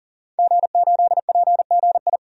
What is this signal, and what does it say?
Morse code, heard as a keyed tone, about 40 wpm, 700 Hz. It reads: G8PGI
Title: DXCC-100 Call Signs - 5 Characters - 40wpm